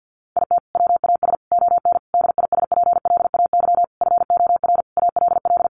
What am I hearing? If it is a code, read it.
Morse code, heard as a keyed tone, about 50 wpm, 700 Hz. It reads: IT WAS ON DISPLAY FOR ALL
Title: Sentences from Top 1200 Words 50wpm